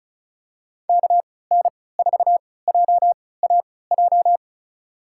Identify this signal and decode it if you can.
Morse code, heard as a keyed tone, about 34 wpm, spaced as for 20 wpm, 700 Hz. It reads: KN4JAJ
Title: USA Call Signs - 6 Characters - 3x Character Spacing - 35wpm